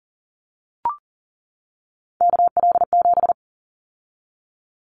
Morse code, heard as a keyed tone, about 40 wpm, 700 Hz. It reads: KL7
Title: Callsign Prefixes for Top 100 DXCC - 40wpm